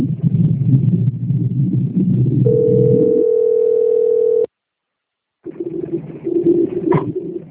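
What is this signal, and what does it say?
Ringback tone (North America), 440 + 480 Hz.
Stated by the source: Outgoing phone call recorded on a PBX system in Bebedouro.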